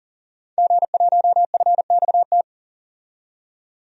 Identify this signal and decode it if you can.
Morse code, heard as a keyed tone, about 38 wpm, 700 Hz. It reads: G1FXT
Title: DXCC-100 Call Signs - 5 Characters - 40wpm